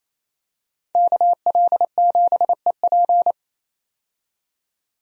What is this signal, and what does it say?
Morse code, heard as a keyed tone, about 28 wpm, 700 Hz. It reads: KL7EP
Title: DXCC-100 Call Signs - 5 Characters - 28wpm